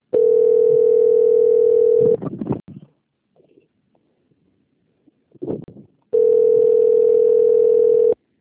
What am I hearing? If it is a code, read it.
Ringback tone (North America), 440 + 480 Hz.
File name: out- - - - - .23183